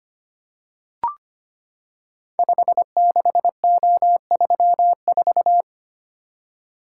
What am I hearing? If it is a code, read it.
Morse code, heard as a keyed tone, about 25 wpm, 700 Hz. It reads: 56O34
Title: Code Groups - 5 Letter:Numbers - 25wpm